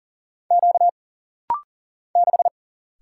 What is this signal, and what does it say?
Morse code, heard as a keyed tone, about 40 wpm, 700 Hz. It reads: Q 6
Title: Single Letter-Number Alt - Rapid-Fire 40wpm